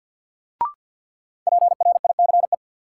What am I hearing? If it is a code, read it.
Morse code, heard as a keyed tone, about 52 wpm, 700 Hz. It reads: PRICE
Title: Sentences from Top 600 Words - New Word Drill - 50wpm